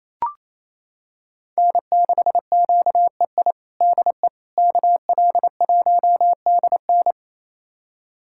Morse code, heard as a keyed tone, about 28 wpm, 700 Hz. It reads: N6QEI DE KL1DN
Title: USA Callsign DE Callsign - 5 Characters - 28wpm